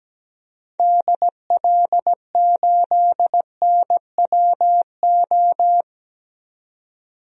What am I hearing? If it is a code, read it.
Morse code, heard as a keyed tone, about 17 wpm, 700 Hz. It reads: DL8NWO
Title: DXCC-100 Call Signs - 6 Characters - 17wpm